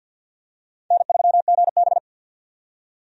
Morse code, heard as a keyed tone, about 47 wpm, 700 Hz. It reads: N3GB
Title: USA Call Signs - 4 Characters - 50wpm